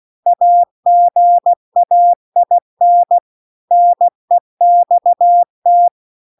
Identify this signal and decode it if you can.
Morse code, heard as a keyed tone, about 16 wpm, 700 Hz. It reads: AGAIN NEXT